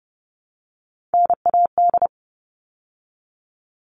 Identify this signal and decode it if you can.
Morse code, heard as a keyed tone, about 30 wpm, 700 Hz. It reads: NAD
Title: All Cut Numbers - 3-Digit 30wpm